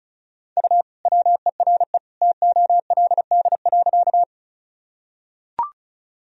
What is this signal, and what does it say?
Morse code, heard as a keyed tone, about 34 wpm, 700 Hz. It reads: U WERE TOLD.
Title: Sentences from Top 400 Words and Abbreviations 35wpm